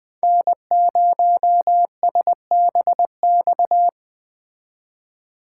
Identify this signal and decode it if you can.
Morse code, heard as a keyed tone, about 20 wpm, 700 Hz. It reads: N0SBX
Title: Speed Racing - USA Call Signs 17wpm